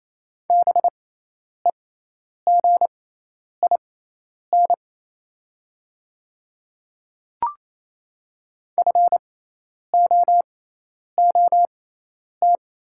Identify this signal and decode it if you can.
Morse code, heard as a keyed tone, about 28 wpm, spaced as for 10 wpm, 700 Hz. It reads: BEGIN FOOT
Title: 6x Character Spacing - Top 500 Words - 28wpm